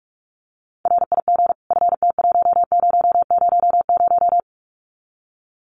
Morse code, heard as a keyed tone, about 44 wpm, 700 Hz. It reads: RIG FT1000
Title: Common QSO Phrases 45wpm